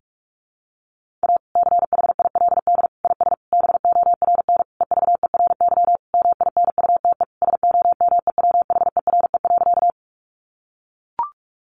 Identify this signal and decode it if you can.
Morse code, heard as a keyed tone, about 44 wpm, 700 Hz. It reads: A CHILD IS BORN EVERY MINUTE SOMEWHERE.
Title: Sentences from Top 1400 Words 45wpm